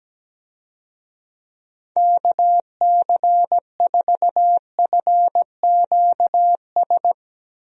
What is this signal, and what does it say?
Morse code, heard as a keyed tone, about 17 wpm, 700 Hz. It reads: KC4FQS
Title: USA Call Signs - 6 Characters - Repeated 3x - 17wpm